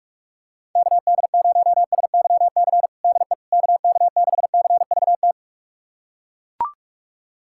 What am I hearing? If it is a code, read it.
Morse code, heard as a keyed tone, about 42 wpm, 700 Hz. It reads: KD0SYC DE KK6CVT
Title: USA Callsign DE Callsign 45wpm